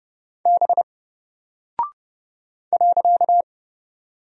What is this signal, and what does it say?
Morse code, heard as a keyed tone, about 30 wpm, 700 Hz. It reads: B .
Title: Single Character (Letter:Number:Punctuation:Prosign) 30wpm